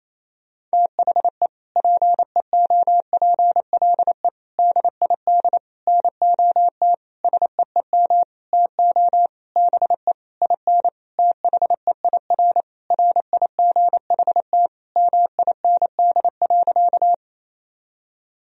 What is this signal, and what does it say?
Morse code, heard as a keyed tone, about 28 wpm, 700 Hz. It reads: THE PEOPLE DID NOT SEEM TO BE IN THEIR RIGHT MIND.
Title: Sentences from Top 1000 Words - Review Part 1 - 28wpm